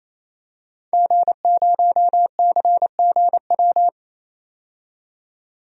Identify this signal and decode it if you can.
Morse code, heard as a keyed tone, about 28 wpm, 700 Hz. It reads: G0CGW
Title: DXCC-100 Call Signs - 5 Characters - 28wpm